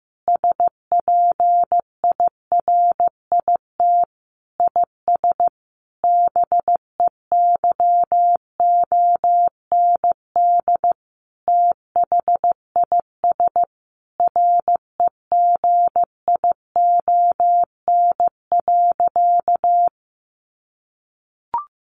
Morse code, heard as a keyed tone, about 15 wpm, 700 Hz. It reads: SPIRIT IS BEYOND THIS REGION.
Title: Sentences from Top 1500 Words 15wpm